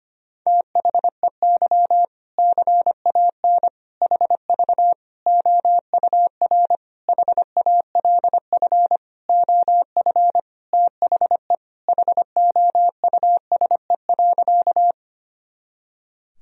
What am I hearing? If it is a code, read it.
Morse code, heard as a keyed tone, about 25 wpm, 700 Hz. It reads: THEY CAN HV OUR HALF OF THE HOUSE.
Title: Sentences from Top 400 Words and Abbreviations 25wpm